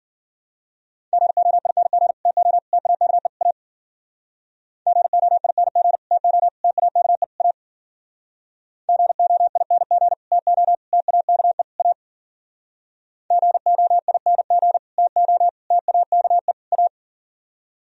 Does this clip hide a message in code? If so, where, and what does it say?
Yes: Morse code from 1.1 to 16.9 s, heard as a keyed tone, about 52 wpm, 700 Hz. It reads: GOING TO TAKE A GOING TO TAKE A GOING TO TAKE A GOING TO TAKE A